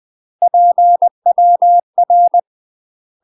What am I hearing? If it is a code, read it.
Morse code, heard as a keyed tone, about 20 wpm, 700 Hz. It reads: PWR
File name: UK MASTER QSOs TRAINING / 20WPM / UK MASTER QSOs 0008